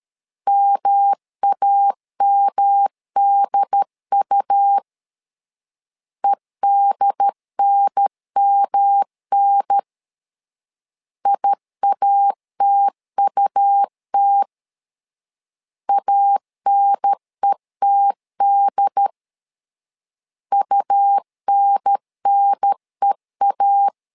Morse code, heard as a keyed tone, about 13 wpm, 800 Hz. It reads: MAMDU EDNMN IATUT ANETD UNNEA
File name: LEZIONI / 3 MWRO / 3 M 12wpm 1541